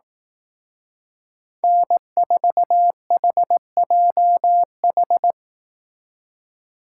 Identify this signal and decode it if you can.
Morse code, heard as a keyed tone, about 18 wpm, 700 Hz. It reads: N4HJH
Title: Speed Racing - USA 1x3 Call Signs 15wpm